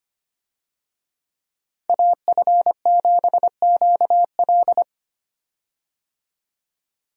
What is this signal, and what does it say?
Morse code, heard as a keyed tone, about 25 wpm, 700 Hz. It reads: AF7QL
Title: USA Call Signs - 5 Characters - 25wpm